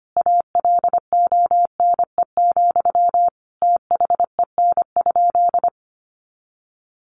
Morse code, heard as a keyed tone, about 25 wpm, 700 Hz. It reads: ALONE, THEN?
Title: Sentences from Top 1000 Words 25wpm